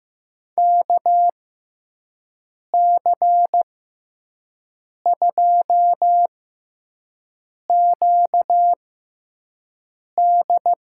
Morse code, heard as a keyed tone, about 16 wpm, spaced as for 5 wpm, 700 Hz. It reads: KC2QD
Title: USA Call Signs - 6 Characters - 6x Character Spacing - 15wpm